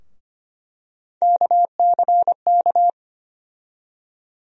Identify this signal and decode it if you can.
Morse code, heard as a keyed tone, about 25 wpm, 700 Hz. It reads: KCK
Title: USA Call Sign Postfixes - 3 Letters 25wpm